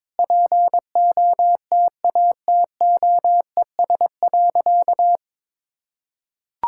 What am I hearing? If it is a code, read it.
Morse code, heard as a keyed tone, about 22 wpm, 700 Hz. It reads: POTATOES.
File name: practice / 22wpm / Easy Sentences with Hints - 22wpm